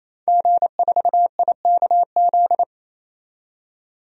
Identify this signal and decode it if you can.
Morse code, heard as a keyed tone, about 28 wpm, 700 Hz. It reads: G4IKZ
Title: DXCC-100 Call Signs - 5 Characters - 28wpm